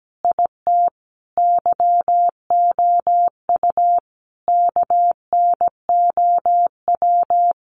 Morse code, heard as a keyed tone, about 17 wpm, 700 Hz. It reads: IT YOU KNOW
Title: Sentences from Top 700 Words 17wpm